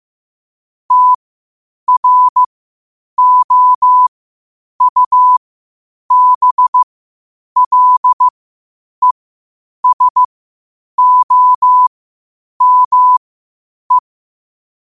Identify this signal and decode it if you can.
Morse code, heard as a keyed tone, about 15 wpm, spaced as for 8 wpm, 1000 Hz. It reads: TROUBLESOME
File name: qotdfiles / 8wpm